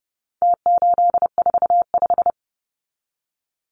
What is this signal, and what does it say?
Morse code, heard as a keyed tone, about 30 wpm, 700 Hz. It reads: T845
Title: Numbers - Contest Style - 4-Digit 30wpm